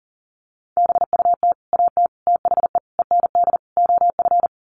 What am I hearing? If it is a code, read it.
Morse code, heard as a keyed tone, about 40 wpm, 700 Hz. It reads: BUT AT THE END OF
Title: Sets of 5 Words Encore2 40wpm